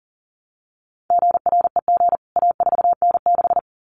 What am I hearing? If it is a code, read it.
Morse code, heard as a keyed tone, about 40 wpm, 700 Hz. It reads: GREG A4N6
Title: CWT Exchange 40wpm